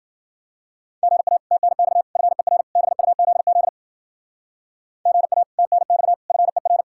Morse code, heard as a keyed tone, about 50 wpm, 700 Hz. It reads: GA TNX FER BUZZ GA TNX FER
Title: Speed-Racing Common QSO Phrases - 40wpm